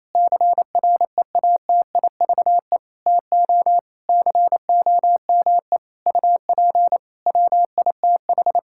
Morse code, heard as a keyed tone, about 28 wpm, 700 Hz. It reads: CREATIVE TO COME UP WITH
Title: Sentences from Top 1000 Words - Review Part 5 - 28wpm